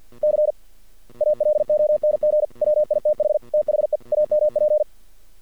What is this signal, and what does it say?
Morse code, heard as a keyed tone, about 48 wpm, 600 Hz. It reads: C ALONG WITH THE AMP
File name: archived / CRSnet